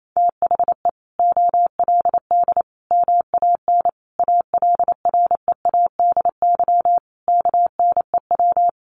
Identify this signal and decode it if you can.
Morse code, heard as a keyed tone, about 28 wpm, 700 Hz. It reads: THE OLD MAN ALREADY KNEW
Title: Sentences from Top 700 Words 28wpm